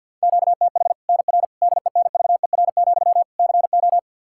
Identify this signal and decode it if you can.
Morse code, heard as a keyed tone, about 51 wpm, 700 Hz. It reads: QTH NR DENVER, CO
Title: K1USN SST QSO Practice 50wpm